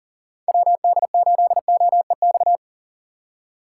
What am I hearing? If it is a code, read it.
Morse code, heard as a keyed tone, about 49 wpm, 700 Hz. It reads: WD8OEX
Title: Speed Racing - USA Call Signs 40wpm